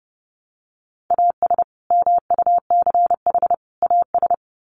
Morse code, heard as a keyed tone, about 30 wpm, 700 Hz. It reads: AS MUCH AS
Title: Sets of 3 Words 30wpm